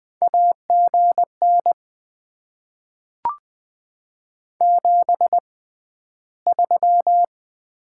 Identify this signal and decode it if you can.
Morse code, heard as a keyed tone, about 20 wpm, spaced as for 12 wpm, 690 Hz. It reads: AGN 73
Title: Top 20 QSO Elements - 6x Character Spacing - 20wpm